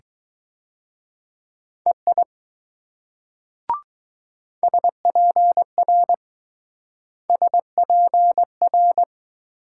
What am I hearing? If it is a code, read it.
Morse code, heard as a keyed tone, about 20 wpm, 700 Hz. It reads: EI SPR SPR